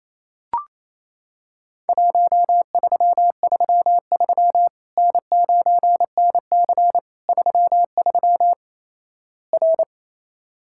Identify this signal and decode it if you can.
Morse code, heard as a keyed tone, about 28 wpm, 700 Hz. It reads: 1333 N9NC 33 R
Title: Worked All Europe DX Contest - QTC Exchange - 28wpm